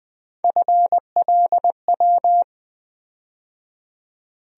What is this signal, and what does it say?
Morse code, heard as a keyed tone, about 20 wpm, 700 Hz. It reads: FLW
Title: USA Call Sign Postfixes - 3 Letters 20wpm